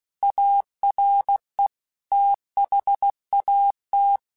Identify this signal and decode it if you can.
Morse code, heard as a keyed tone, about 16 wpm, 800 Hz. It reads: ARE THAT
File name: Technical Subjects / Technical Subjects-mp3s / Reactance @15